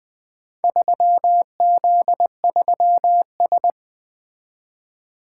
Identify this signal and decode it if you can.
Morse code, heard as a keyed tone, about 20 wpm, 700 Hz. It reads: 3Z3S
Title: Four Letter-Number Combinations - 20wpm